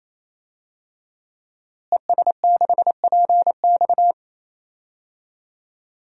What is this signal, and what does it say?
Morse code, heard as a keyed tone, about 28 wpm, 700 Hz. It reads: ES6PX
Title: DXCC-100 Call Signs - 5 Characters - Repeated 3x - 28wpm